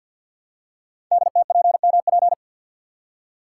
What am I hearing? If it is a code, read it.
Morse code, heard as a keyed tone, about 47 wpm, 700 Hz. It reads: DTPMP
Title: Code Groups - 5 Letters - 50wpm